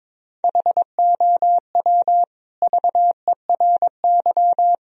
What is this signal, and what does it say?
Morse code, heard as a keyed tone, about 22 wpm, 700 Hz. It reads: HOW VERY
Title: Sentences from Top 1000 Words - Review Part 1 - 22wpm